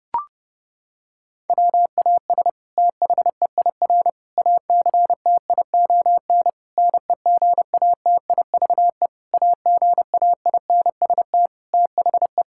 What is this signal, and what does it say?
Morse code, heard as a keyed tone, about 30 wpm, 700 Hz. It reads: WAS THEIR ACTION NEGATIVE AGAINST THE
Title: Sentences from Top 1000 Words - Review Part 6 - 30wpm